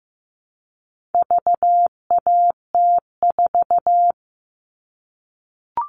Morse code, heard as a keyed tone, about 15 wpm, 700 Hz. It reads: VAT4
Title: All Cut Numbers - 4-Digit 15wpm